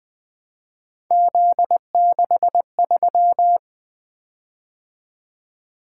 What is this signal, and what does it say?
Morse code, heard as a keyed tone, about 20 wpm, 700 Hz. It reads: Z63
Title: Three Letter-Number Combinations - 20wpm